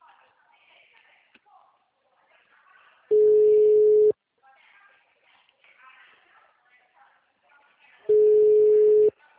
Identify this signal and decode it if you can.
Ringback tone (Europe, 425 Hz), with speech.